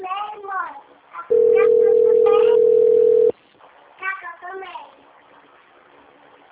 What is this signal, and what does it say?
Ringback tone (North America), 440 + 480 Hz, with speech.
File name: rg-1600-unknown- - - .32599